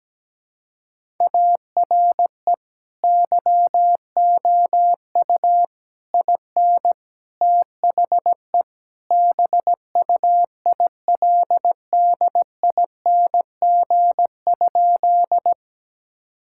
Morse code, heard as a keyed tone, about 17 wpm, 700 Hz. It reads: ARE YOU IN THE BUILDING?
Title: Sentences from Top 1000 Words - Review Part 5 - 17wpm